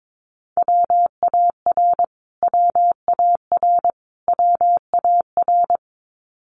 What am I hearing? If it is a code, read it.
Morse code, heard as a keyed tone, about 22 wpm, 700 Hz. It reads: WAR WAR WAR
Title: Top 500 Words Repeated 22wpm